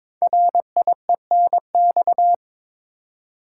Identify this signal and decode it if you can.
Morse code, heard as a keyed tone, about 22 wpm, 700 Hz. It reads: RIENX
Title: Code Groups - 5 Letters - 22wpm